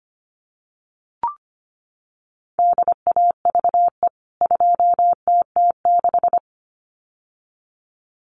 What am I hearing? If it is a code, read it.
Morse code, heard as a keyed tone, about 25 wpm, 700 Hz. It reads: DAVE 2TT6
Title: CWT Exchange 25wpm